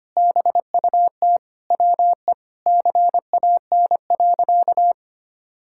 Morse code, heard as a keyed tone, about 25 wpm, 700 Hz. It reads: BUT WE CAN.
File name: practice / 25wpm / Sentences with 2-Word Phrases - 25wpm